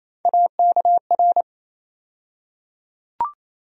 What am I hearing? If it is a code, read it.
Morse code, heard as a keyed tone, about 28 wpm, 700 Hz. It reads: AKR
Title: USA Call Sign Postfixes - 3 Letters 28wpm